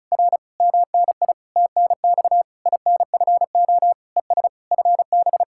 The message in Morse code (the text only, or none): R MNI TNX INFO ES FB